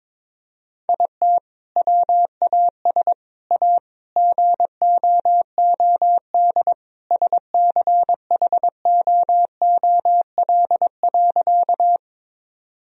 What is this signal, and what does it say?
Morse code, heard as a keyed tone, about 22 wpm, 700 Hz. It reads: IT WAS A GOOD SCHOOL.
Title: Sentences from Top 1000 Words - Review Part 1 - 22wpm